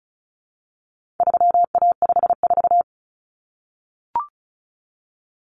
Morse code, heard as a keyed tone, about 34 wpm, 700 Hz. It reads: 3A54
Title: Common Cut Numbers - 4-Digit 35wpm